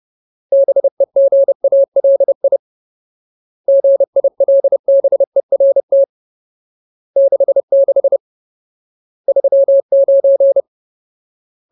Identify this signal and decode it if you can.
Morse code, heard as a keyed tone, about 30 wpm, 550 Hz. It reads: BEGALI GILBERT 66 39
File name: Ultimate Word List / 30WPM Ultimate